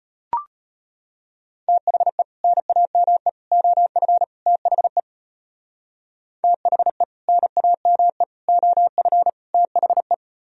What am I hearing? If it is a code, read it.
Morse code, heard as a keyed tone, about 33 wpm, 700 Hz. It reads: THE NAME OF THE THE NAME OF THE